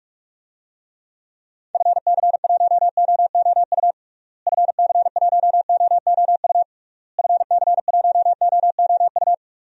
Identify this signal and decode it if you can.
Morse code, heard as a keyed tone, about 46 wpm, 700 Hz. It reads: FC1OOU FC1OOU FC1OOU
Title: DXCC-100 Call Signs - 6 Characters - Repeated 3x - 45wpm